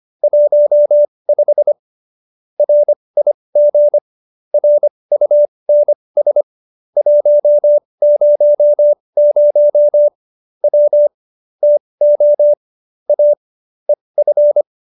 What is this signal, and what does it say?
Morse code, heard as a keyed tone, about 25 wpm, spaced as for 20 wpm, 590 Hz. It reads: 15 RIG RUNS 100 W TO A EF
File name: QSO303 20